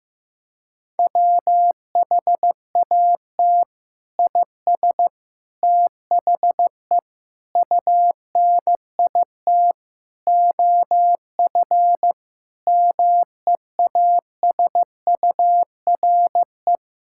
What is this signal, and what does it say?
Morse code, heard as a keyed tone, about 15 wpm, 700 Hz. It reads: WHAT IS THE UNIT OF MEASURE
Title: Sentences from Top 1000 Words - Review Part 6 - 15wpm